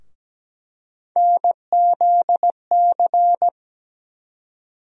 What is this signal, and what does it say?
Morse code, heard as a keyed tone, about 17 wpm, 700 Hz. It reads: NZC
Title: USA Call Sign Postfixes - 3 Letters 17wpm